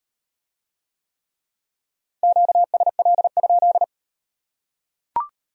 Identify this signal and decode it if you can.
Morse code, heard as a keyed tone, about 36 wpm, 700 Hz. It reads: QSL?